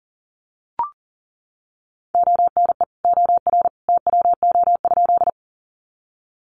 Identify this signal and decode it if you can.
Morse code, heard as a keyed tone, about 40 wpm, 700 Hz. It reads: ONE OR TWO?
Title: Sentences from Top 100 Words 40wpm